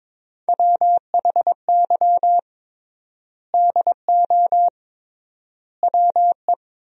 Morse code, heard as a keyed tone, about 22 wpm, 700 Hz. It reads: WHY DO WE
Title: 3x Word-Spacing - Sentences from Top 400 Words - 22wpm